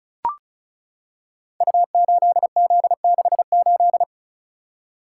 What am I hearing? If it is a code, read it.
Morse code, heard as a keyed tone, about 36 wpm, 700 Hz. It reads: U8Z68
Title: Code Groups - 5 Letter:Numbers - 35wpm